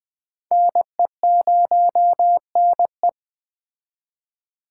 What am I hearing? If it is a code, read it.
Morse code, heard as a keyed tone, about 21 wpm, 700 Hz. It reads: NE0NE